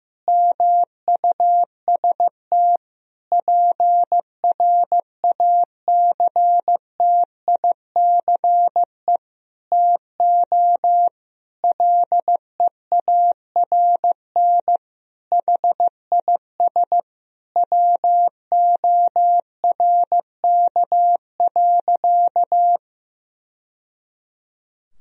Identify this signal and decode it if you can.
Morse code, heard as a keyed tone, about 16 wpm, 700 Hz. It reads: MUST PRACTICE TO LEARN HIS WORK.
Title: Sentences from Top 1000 Words - Review Part 4 - 15wpm